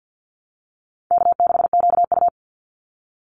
Morse code, heard as a keyed tone, about 50 wpm, 700 Hz. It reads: K6QU
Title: Easy Call Signs Repeated 3x - 50wpm